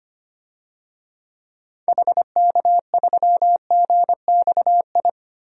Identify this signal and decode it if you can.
Morse code, heard as a keyed tone, about 25 wpm, 700 Hz. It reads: HK3GXI
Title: DXCC-100 Call Signs - 6 Characters - 25wpm